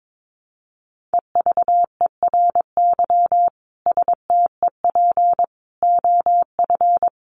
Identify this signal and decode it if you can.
Morse code, heard as a keyed tone, about 22 wpm, 700 Hz. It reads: EVERY STEP OF
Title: Sets of 5 Words Encore2 22wpm